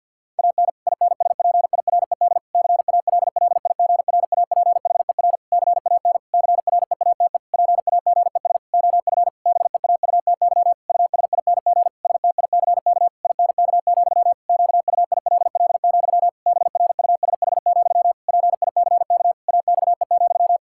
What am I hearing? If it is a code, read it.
Morse code, heard as a keyed tone, about 52 wpm, 700 Hz. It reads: AN INSPIRED CALLIGRAPHER CAN CREATE PAGES OF BEAUTY USING STICK INK, QUILL, BRUSH, PICK AXE,
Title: Pangram Challenge v2 - 50wpm